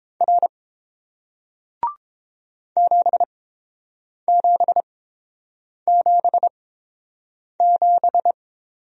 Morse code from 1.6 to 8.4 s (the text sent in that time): 7777